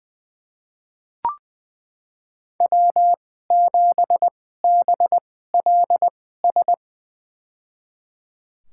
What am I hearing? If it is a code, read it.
Morse code, heard as a keyed tone, about 20 wpm, spaced as for 14 wpm, 700 Hz. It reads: W7BLS
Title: USA Call Signs - 5 Characters - 2x Character Spacing - 20wpm